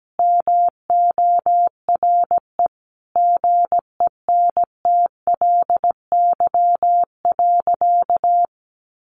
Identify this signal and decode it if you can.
Morse code, heard as a keyed tone, about 17 wpm, 700 Hz. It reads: MORE GENTLY.
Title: Sentences from Top 1500 Words 17wpm